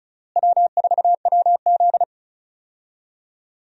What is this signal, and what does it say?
Morse code, heard as a keyed tone, about 36 wpm, 700 Hz. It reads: W4WZ
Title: USA Call Signs - 4 Characters - Repeated 3x - 35wpm